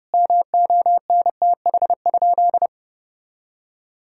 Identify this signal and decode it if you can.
Morse code, heard as a keyed tone, about 29 wpm, 700 Hz. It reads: MONTH?
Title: Sentences from Top 1000 Words - Review Part 6 - 30wpm